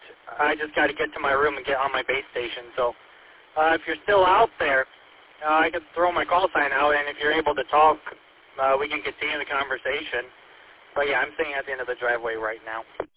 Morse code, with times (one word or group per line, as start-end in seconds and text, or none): none